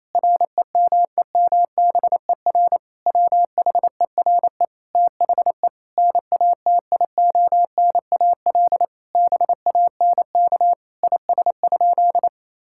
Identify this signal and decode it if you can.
Morse code, heard as a keyed tone, about 28 wpm, 700 Hz. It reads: REMEMBER WHERE THE NATIONAL BANK IS?
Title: Sentences from Top 1000 Words - Review Part 6 - 28wpm